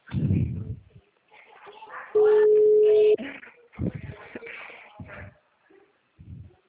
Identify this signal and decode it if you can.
Ringback tone (Europe, 425 Hz), with speech.